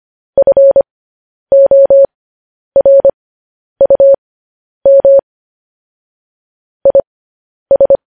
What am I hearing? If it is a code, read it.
Morse code, heard as a keyed tone, about 25 wpm, spaced as for 10 wpm, 550 Hz. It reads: FORUM IS
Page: news headlines in morse code at 10 wpm